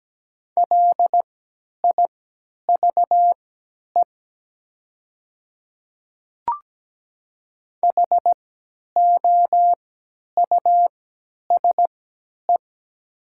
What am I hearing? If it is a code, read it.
Morse code, heard as a keyed tone, about 17 wpm, spaced as for 10 wpm, 700 Hz. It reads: LIVE HOUSE
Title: 3x Character Spacing - Top 300 Words - 17wpm